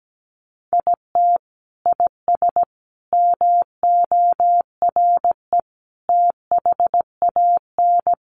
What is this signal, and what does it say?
Morse code, heard as a keyed tone, about 17 wpm, 700 Hz. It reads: IT IS MORE THAN
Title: Sentences from Top 1400 Words 17wpm